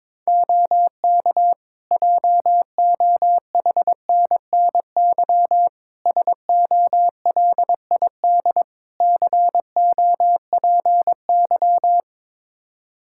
Morse code, heard as a keyed tone, about 22 wpm, 700 Hz. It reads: OK JOHNNY SOLID COPY
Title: Practice QSO - Exchange 3 - More Info - 22wpm